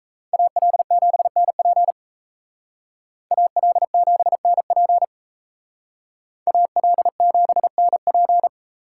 Morse code, heard as a keyed tone, about 36 wpm, 700 Hz. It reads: AL7NP AL7NP AL7NP